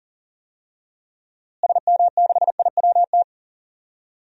Morse code, heard as a keyed tone, about 38 wpm, 700 Hz. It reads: SM6IWT
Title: DXCC-100 Call Signs - 6 Characters - Repeated 2x - 40wpm